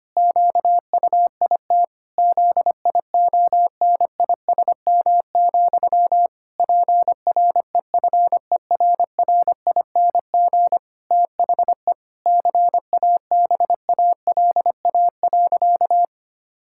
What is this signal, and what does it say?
Morse code, heard as a keyed tone, about 25 wpm, 700 Hz. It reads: QUIT ZIONISM, PREFERRING THE CABALA.
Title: Pangram Challenge v2 - 25wpm